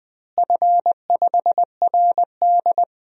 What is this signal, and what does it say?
Morse code, heard as a keyed tone, about 20 wpm, 700 Hz. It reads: F5RD
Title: DXCC-100 Call Signs - 4 Characters - Repeated 2x - 20wpm